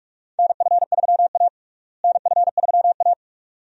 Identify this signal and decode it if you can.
Morse code, heard as a keyed tone, about 45 wpm, 700 Hz. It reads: NF3A NF3A
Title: USA Call Signs - 4 Characters - Repeated 2x - 45wpm